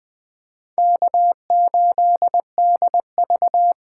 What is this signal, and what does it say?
Morse code, heard as a keyed tone, about 24 wpm, 700 Hz. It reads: K8DV
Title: Speed Racing - USA 1x2 Call Signs 20wpm